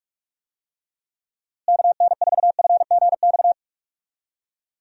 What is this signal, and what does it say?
Morse code, heard as a keyed tone, about 46 wpm, 700 Hz. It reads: KN4FGX
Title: USA Call Signs - 6 Characters - Repeated 2x - 45wpm